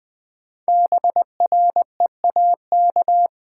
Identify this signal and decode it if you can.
Morse code, heard as a keyed tone, about 20 wpm, 700 Hz. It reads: BREAK
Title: Sentences from Top 900 Words - New Word Drill - 20wpm